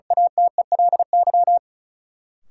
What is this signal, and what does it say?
Morse code, heard as a keyed tone, about 36 wpm, 700 Hz. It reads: ATELY
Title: Sentences from Top 1100 Words - New Word Drill Alt - 35wpm